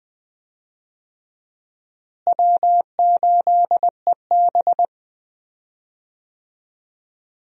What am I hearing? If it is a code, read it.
Morse code, heard as a keyed tone, about 20 wpm, 700 Hz. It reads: W8EB
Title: USA Call Signs - 4 Characters - Repeated 3x - 20wpm